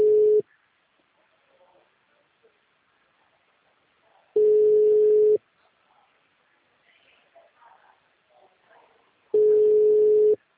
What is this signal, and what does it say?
Ringback tone (Europe, 425 Hz).